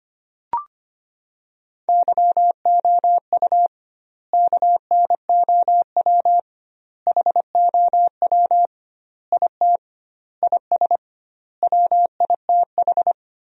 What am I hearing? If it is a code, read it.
Morse code, heard as a keyed tone, about 25 wpm, 700 Hz. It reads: YOU KNOW HOW IT IS WITH
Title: 2x Word-Spacing - Sentences from Top 100 Words - 25wpm